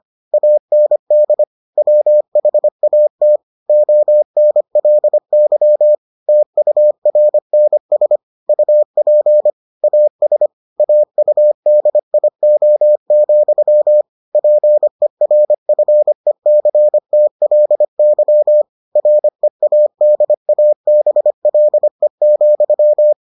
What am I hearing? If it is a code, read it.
Morse code, heard as a keyed tone, about 25 wpm, 600 Hz. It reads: AND WHAT ONLY TURNS UP AS AUDIO, PERFECTLY READABLE,